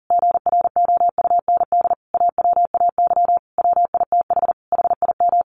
Morse code, heard as a keyed tone, about 40 wpm, 700 Hz. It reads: GROUND AWAY WITH HIM
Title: Sentences from Top 1100 Words 40wpm